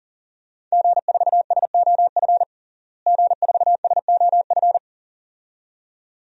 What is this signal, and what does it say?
Morse code, heard as a keyed tone, about 40 wpm, 700 Hz. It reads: G4SOF G4SOF
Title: DXCC-100 Call Signs - 5 Characters - Repeated 2x - 40wpm